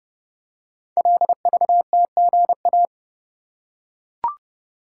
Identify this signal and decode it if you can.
Morse code, heard as a keyed tone, about 30 wpm, 700 Hz. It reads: LVTGA
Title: Code Groups - 5 Letters - 30wpm